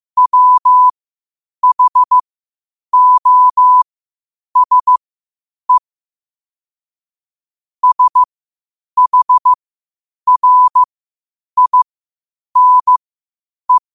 Morse code, heard as a keyed tone, about 15 wpm, spaced as for 8 wpm, 1000 Hz. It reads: WHOSE SHRINE
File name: qotdfiles / 8wpm